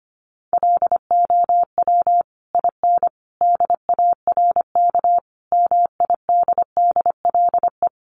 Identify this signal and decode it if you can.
Morse code, heard as a keyed tone, about 25 wpm, 700 Hz. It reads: LOW IN DARK MIDDLE
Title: Sentences from Top 900 Words 25wpm